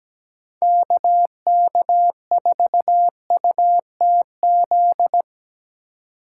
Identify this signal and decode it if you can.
Morse code, heard as a keyed tone, about 19 wpm, 700 Hz. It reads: KK4UTZ
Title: Speed Racing - USA 2x3 Call Signs 17wpm